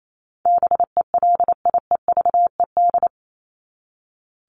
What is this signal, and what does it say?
Morse code, heard as a keyed tone, about 28 wpm, 700 Hz. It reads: BELIEVED
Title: ED Suffix - Top 100 Words with Base Words 28wpm